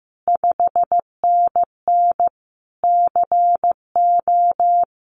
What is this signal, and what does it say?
Morse code, heard as a keyed tone, about 15 wpm, 700 Hz. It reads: 5NN CO
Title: Common QSO Phrases Encore 15wpm